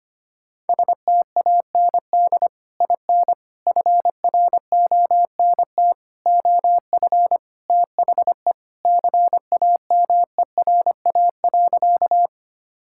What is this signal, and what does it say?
Morse code, heard as a keyed tone, about 25 wpm, 700 Hz. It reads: STAND IN FRONT OF THE CAMERA.
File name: practice / 25wpm / Sentences with 3-Word Phrases - 25wpm